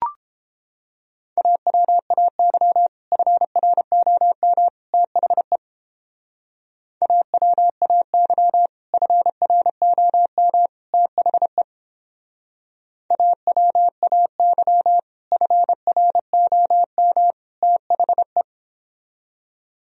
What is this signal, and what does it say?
Morse code, heard as a keyed tone, about 29 wpm, 700 Hz. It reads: AWAY FROM THE AWAY FROM THE AWAY FROM THE